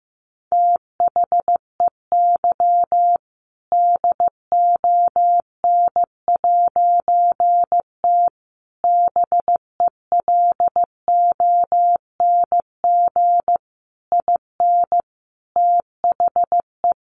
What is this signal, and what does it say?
Morse code, heard as a keyed tone, about 15 wpm, 700 Hz. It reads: THEY DON'T BELONG IN THE
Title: A Sound Of Thunder 15wpm